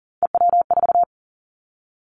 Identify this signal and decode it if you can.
Morse code, heard as a keyed tone, about 40 wpm, 700 Hz. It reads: EW4
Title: Callsign Prefixes for Top 100 DXCC - 40wpm